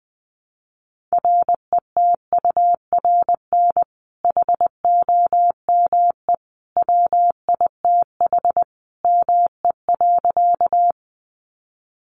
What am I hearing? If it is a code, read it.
Morse code, heard as a keyed tone, about 20 wpm, 700 Hz. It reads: RETURN HOME WITH ME.
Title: Sentences from Top 800 Words 20wpm